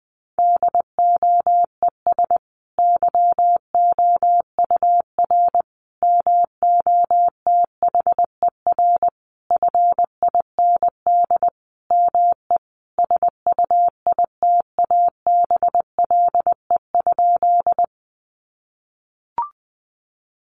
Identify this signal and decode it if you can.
Morse code, heard as a keyed tone, about 20 wpm, 700 Hz. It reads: DOES YOUR MOTHER FIND ME SUITABLE?
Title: Sentences from Top 1400 Words 20wpm